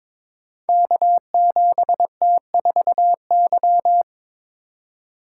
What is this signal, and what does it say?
Morse code, heard as a keyed tone, about 22 wpm, 700 Hz. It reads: K7T4Y
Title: Code Groups - 5 Letter:Numbers - 22wpm